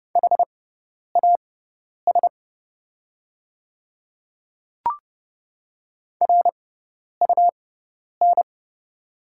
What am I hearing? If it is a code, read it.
Morse code, heard as a keyed tone, about 30 wpm, spaced as for 10 wpm, 700 Hz. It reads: HAS RUN
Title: 6x Character Spacing - Top 300 Words - 30wpm